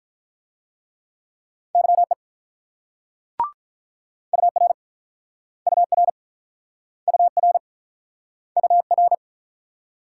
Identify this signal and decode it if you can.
Morse code, heard as a keyed tone, about 48 wpm, 700 Hz. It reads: YE UR UR UR UR